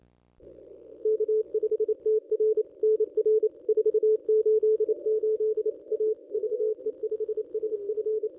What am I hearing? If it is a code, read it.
Morse code, heard as a keyed tone, about 28 wpm, 430 Hz. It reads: K5TR NR 488 A VE5SF
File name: contests / ss-cw-k5tr-op-n5rz -1200